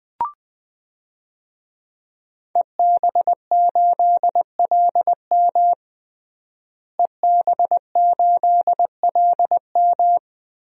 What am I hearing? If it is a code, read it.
Morse code, heard as a keyed tone, about 20 wpm, 700 Hz. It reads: EB8LM EB8LM
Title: DXCC-100 Call Signs - 5 Characters - Repeated 3x - 20wpm